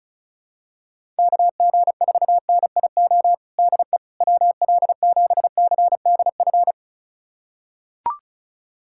Morse code, heard as a keyed tone, about 36 wpm, 700 Hz. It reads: KG4NIO DE WL7CDF
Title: USA Callsign DE Callsign - 6 Characters - 35wpm